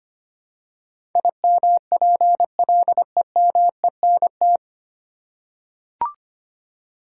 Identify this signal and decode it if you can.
Morse code, heard as a keyed tone, about 25 wpm, 700 Hz. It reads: IMPLEMENT
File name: Nine Letter Words - 25wpm